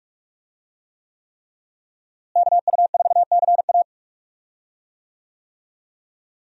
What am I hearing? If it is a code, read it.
Morse code, heard as a keyed tone, about 41 wpm, 700 Hz. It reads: KU4CA
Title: USA Call Signs - 5 Characters - Repeated 3x - 45wpm